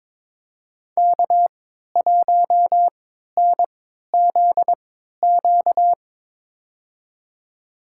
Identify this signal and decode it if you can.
Morse code, heard as a keyed tone, about 22 wpm, spaced as for 13 wpm, 700 Hz. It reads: K1NZQ
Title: USA Call Signs - 3x Character Spacing - 22wpm